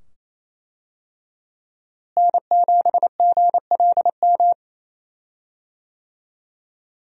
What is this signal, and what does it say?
Morse code, heard as a keyed tone, about 28 wpm, 700 Hz. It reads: N7GLM
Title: USA Call Signs - 5 Characters - Repeated 2x - 28wpm